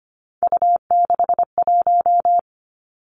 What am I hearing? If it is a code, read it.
Morse code, heard as a keyed tone, about 25 wpm, 700 Hz. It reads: U61
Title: Three Challenging Letter-Number Combinations 25wpm